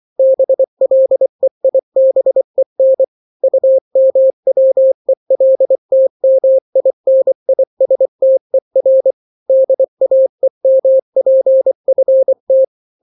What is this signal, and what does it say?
Morse code, heard as a keyed tone, about 24 wpm, 530 Hz. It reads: BLEIBEN UMWELTMINISTER DAEMPFT
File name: daily / 23wpm